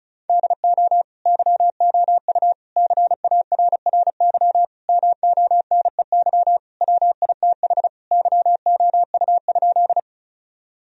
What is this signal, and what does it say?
Morse code, heard as a keyed tone, about 36 wpm, 700 Hz. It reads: DO YOU CARRY MONEY WITH YOU?
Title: Sentences from Top 1000 Words - Review Part 1 - 35wpm